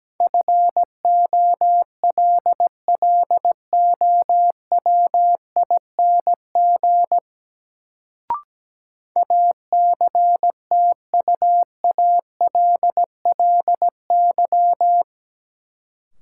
Morse code, heard as a keyed tone, about 17 wpm, 700 Hz. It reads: FOLLOWING ACTUALLY
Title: Sentences from Top 700 Words - New Word Drill - 17wpm